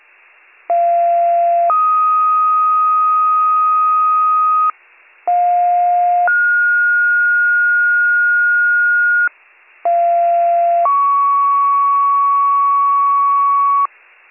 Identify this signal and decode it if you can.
Two-tone page, heard as a one-second tone, then a three-second tone, about 690 then 1250 Hz, about 710 then 1430 Hz, about 690 then 1120 Hz. It is heard from 0.7 to 13.9 s.